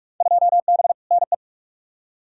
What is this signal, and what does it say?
Morse code, heard as a keyed tone, about 42 wpm, 700 Hz. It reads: 2B NE
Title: Field Day Exchanges Encore - 45wpm